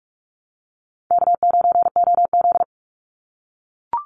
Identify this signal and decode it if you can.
Morse code, heard as a keyed tone, about 44 wpm, 700 Hz. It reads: K9OZ
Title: NA SOTA Chaser Callsigns 45wpm